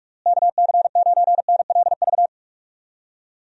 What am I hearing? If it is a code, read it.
Morse code, heard as a keyed tone, about 46 wpm, 700 Hz. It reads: KC9NLV
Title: USA Call Signs - 6 Characters - Repeated 3x - 45wpm